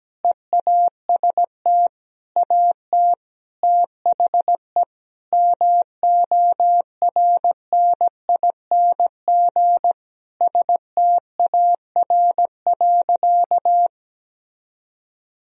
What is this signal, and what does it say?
Morse code, heard as a keyed tone, about 17 wpm, 700 Hz. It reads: EAST AT THE MORNING STAR.
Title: Sentences from Top 1000 Words - Review Part 3 - 17wpm